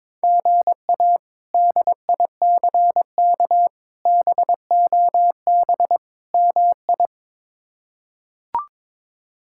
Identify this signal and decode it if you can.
Morse code, heard as a keyed tone, about 21 wpm, 700 Hz. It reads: GA DICK BOB MI
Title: K1USN SST QSO Practice - 22wpm